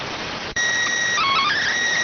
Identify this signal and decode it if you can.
SSTV (slow-scan television), PD 290.